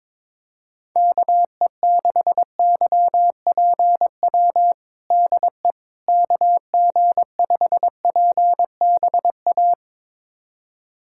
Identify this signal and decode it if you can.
Morse code, heard as a keyed tone, about 21 wpm, 700 Hz. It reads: KE6YPW DE KG5PBA
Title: USA Callsign DE Callsign - 6 Characters - 22wpm